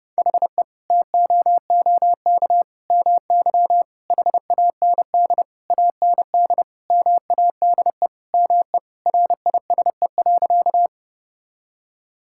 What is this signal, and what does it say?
Morse code, heard as a keyed tone, about 30 wpm, 700 Hz. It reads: HE TOOK MY HAND AND MADE ME RISE.
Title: Sentences from Top 1000 Words - Review Part 6 - 30wpm